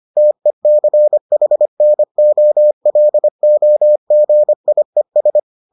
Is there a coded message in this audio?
Yes: Morse code from 0.2 to 5.4 s, heard as a keyed tone, about 25 wpm, 600 Hz. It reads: TECHNOLOGIES